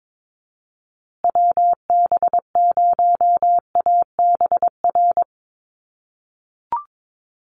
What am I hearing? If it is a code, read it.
Morse code, heard as a keyed tone, about 22 wpm, 700 Hz. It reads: WB0ABR
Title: USA Call Signs Repeated 3x 22wpm